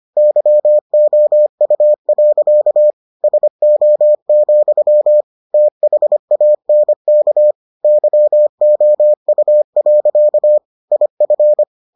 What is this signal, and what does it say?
Morse code, heard as a keyed tone, about 25 wpm, 600 Hz. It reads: YOU. SO, THANK YOU. IF